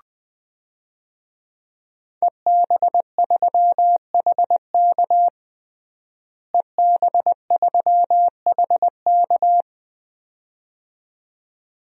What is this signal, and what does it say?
Morse code, heard as a keyed tone, about 20 wpm, 700 Hz. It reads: EB3HK EB3HK
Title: DXCC-100 Call Signs - 5 Characters - Repeated 2x - 20wpm